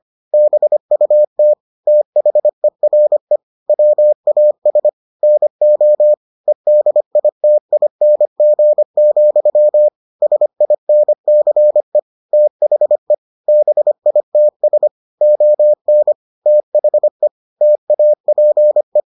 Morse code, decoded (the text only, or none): BUT THERE WAS NO EDITING, SINCE THE BITS ON THE TAPE